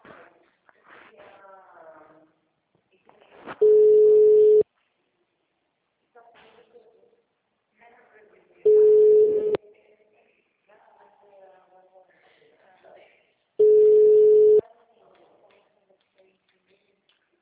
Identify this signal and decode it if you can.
Ringback tone (Europe, 425 Hz), with speech.